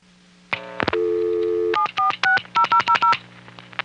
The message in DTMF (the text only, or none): 7460000